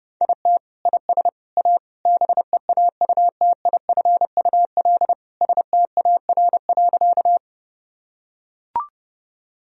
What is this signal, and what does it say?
Morse code, heard as a keyed tone, about 30 wpm, 700 Hz. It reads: IT IS A BEAUTIFUL STAR.
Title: Sentences from Top 1000 Words - Review Part 2 - 30wpm